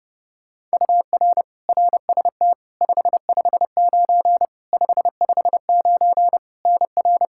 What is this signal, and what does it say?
Morse code, heard as a keyed tone, about 30 wpm, 700 Hz. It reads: UR RST 559 559 NR
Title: Practice QSO - Exchange 1 - Beginning - 30wpm